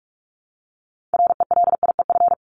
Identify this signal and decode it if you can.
Morse code, heard as a keyed tone, about 44 wpm, 700 Hz. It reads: RELIEF
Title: Top 1501-1750 Words 45wpm